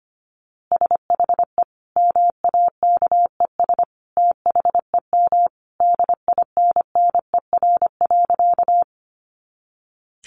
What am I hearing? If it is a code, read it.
Morse code, heard as a keyed tone, about 25 wpm, 700 Hz. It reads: SHE MAKES THEM DINNER.
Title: Easy Phrases 25wpm